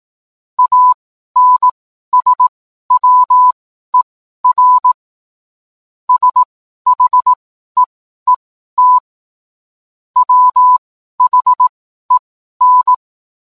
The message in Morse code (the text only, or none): ANSWER SHEET WHEN